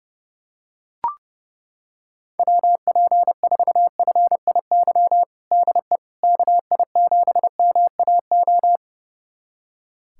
Morse code, heard as a keyed tone, about 30 wpm, 700 Hz. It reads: WP4FIY DE KI7MAO
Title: USA Callsign DE Callsign 30wpm